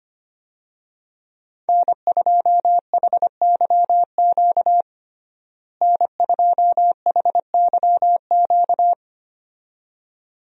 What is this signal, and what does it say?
Morse code, heard as a keyed tone, about 25 wpm, 700 Hz. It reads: N2HYQ N2HYQ
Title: USA Call Signs - 5 Characters - Repeated 2x - 25wpm